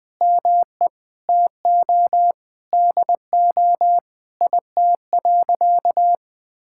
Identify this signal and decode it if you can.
Morse code, heard as a keyed tone, about 21 wpm, 700 Hz. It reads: ME TO DO IT.
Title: Sentences from Top 1000 Words - Review Part 5 - 20wpm